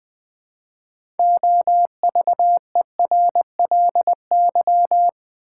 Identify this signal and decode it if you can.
Morse code, heard as a keyed tone, about 20 wpm, 700 Hz. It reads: OVERLY
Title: Build A Word - 3 Letters At A Time - 6-Letter Words - 20wpm